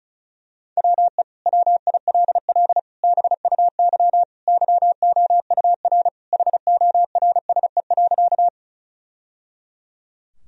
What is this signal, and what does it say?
Morse code, heard as a keyed tone, about 36 wpm, 700 Hz. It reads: WE WILL BUY YOUR HORSE.
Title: Sentences from Top 1000 Words - Review Part 4 - 35wpm